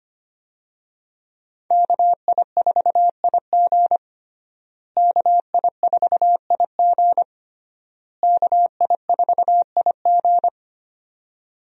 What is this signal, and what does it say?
Morse code, heard as a keyed tone, about 25 wpm, 700 Hz. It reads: KI4IG KI4IG KI4IG
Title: USA Call Signs - 5 Characters - Repeated 3x - 25wpm